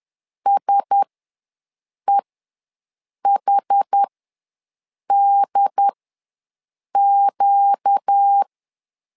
Morse code, heard as a keyed tone, about 10 wpm, spaced as for 6 wpm, 800 Hz. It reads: SEHDQ
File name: LEZIONI / 4 Q1S ws 10wpm 2208